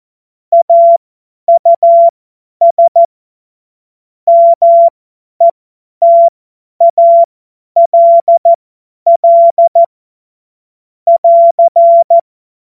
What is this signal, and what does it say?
Morse code, heard as a keyed tone, about 14 wpm, spaced as for 10 wpm, 680 Hz. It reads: AUS METALL <AR>
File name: Morsekurs / Witze / Witz 17.Oktober 2023 50 F70